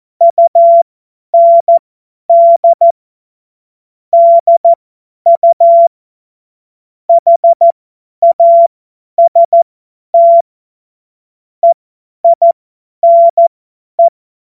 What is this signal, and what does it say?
Morse code, heard as a keyed tone, about 14 wpm, spaced as for 10 wpm, 680 Hz. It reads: UND DU HAST EINE
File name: Morsekurs / Witze / Witz 25. Oktober 2022 50 F70